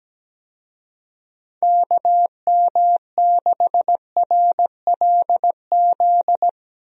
Morse code, heard as a keyed tone, about 17 wpm, 700 Hz. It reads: KM6RLZ
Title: USA Call Signs - 6 Characters - Repeated 3x - 17wpm